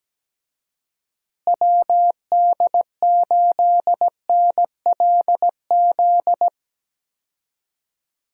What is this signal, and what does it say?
Morse code, heard as a keyed tone, about 17 wpm, 700 Hz. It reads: WD8NLZ
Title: USA Call Signs - 6 Characters - Repeated 2x - 17wpm